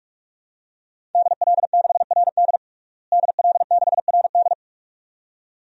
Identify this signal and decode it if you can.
Morse code, heard as a keyed tone, about 46 wpm, 700 Hz. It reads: DL6RD DL6RD
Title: DXCC-100 Call Signs - 5 Characters - Repeated 2x - 45wpm